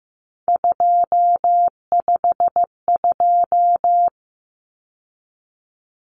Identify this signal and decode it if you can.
Morse code, heard as a keyed tone, about 15 wpm, 700 Hz. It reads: 252
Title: Common Cut Numbers - 3-Digit 15wpm